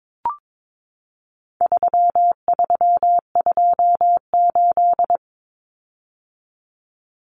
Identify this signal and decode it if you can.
Morse code, heard as a keyed tone, about 22 wpm, 700 Hz. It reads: 3328
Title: Common Cut Numbers - 4-Digit 22wpm